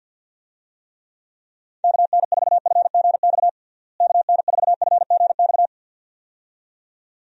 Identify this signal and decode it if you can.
Morse code, heard as a keyed tone, about 51 wpm, 700 Hz. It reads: KN4FGX KN4FGX
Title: USA Call Signs - 6 Characters - Repeated 2x - 50wpm